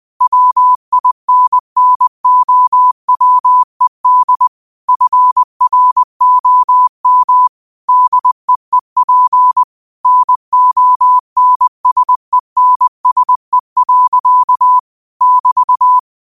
Morse code, heard as a keyed tone, about 20 wpm, 1000 Hz. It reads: WINNOWED FROM DEEP NONSENSE. <BT>
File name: qotdfiles / 20wpm